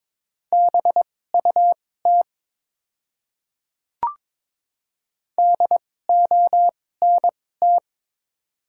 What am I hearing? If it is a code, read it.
Morse code, heard as a keyed tone, about 22 wpm, spaced as for 16 wpm, 700 Hz. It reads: BUT DONT
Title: 2x Character Spacing - Top 500 Words - 22wpm